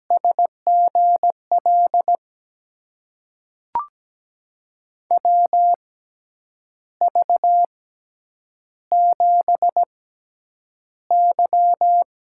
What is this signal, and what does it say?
Morse code, heard as a keyed tone, about 17 wpm, spaced as for 8 wpm, 700 Hz. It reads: SGL WV7Y
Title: USA Call Signs - 6x Character Spacing - 17wpm